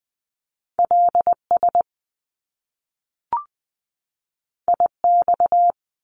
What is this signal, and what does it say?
Morse code, heard as a keyed tone, about 20 wpm, 700 Hz. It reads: LS IX
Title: Top Two Letter Combinations in Words 20wpm